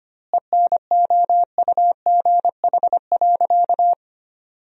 Morse code, heard as a keyed tone, about 25 wpm, 700 Hz. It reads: ENOUGH.
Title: 2x Word-Spacing - Sentences from Top 400 Words - 25wpm